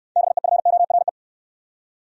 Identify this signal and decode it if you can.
Morse code, heard as a keyed tone, about 68 wpm, 700 Hz. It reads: BEFORE